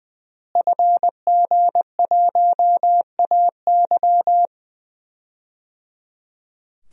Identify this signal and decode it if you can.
Morse code, heard as a keyed tone, about 21 wpm, 700 Hz. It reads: FG1AY
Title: DXCC-100 Call Signs - 5 Characters - Repeated 3x - 20wpm